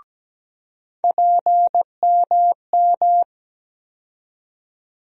Morse code, heard as a keyed tone, about 17 wpm, 700 Hz. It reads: PMM
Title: USA Call Sign Postfixes - 3 Letters 17wpm